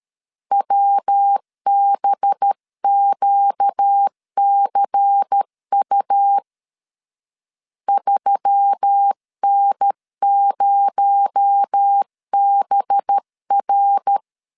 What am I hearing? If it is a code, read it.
Morse code, heard as a keyed tone, about 13 wpm, 800 Hz. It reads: WBQCU 3N0BR
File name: LEZIONI / 7 GF9Z / 7 GF9Z 12wpm 2147